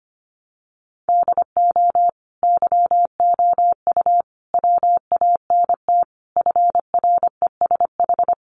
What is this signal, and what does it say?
Morse code, heard as a keyed tone, about 25 wpm, 700 Hz. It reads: DO YOU WANT FRESH
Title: Sentences from Top 1200 Words 25wpm